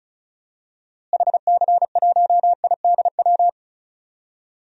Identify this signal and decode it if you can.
Morse code, heard as a keyed tone, about 36 wpm, 700 Hz. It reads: HC1IDW
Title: DXCC-100 Call Signs - 6 Characters - Repeated 3x - 35wpm